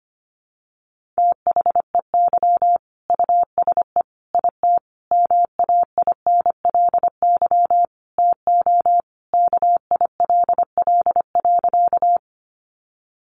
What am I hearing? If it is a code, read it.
Morse code, heard as a keyed tone, about 25 wpm, 700 Hz. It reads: THEY USE IT MAINLY TO KILL.
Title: Sentences from Top 1400 Words 25wpm